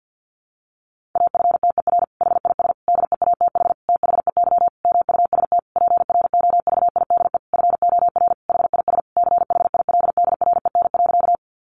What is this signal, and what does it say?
Morse code, heard as a keyed tone, about 48 wpm, 700 Hz. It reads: AFTER HIS DEATH THEY MUST PROVIDE FOR HIS CHILDREN.
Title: Sentences from Top 1100 Words 50wpm